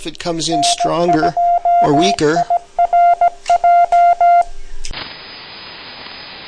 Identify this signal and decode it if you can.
Morse code, heard as a keyed tone, about 17 wpm, 700 Hz, with speech. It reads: N2IRJ
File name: 3995 kHz, LSB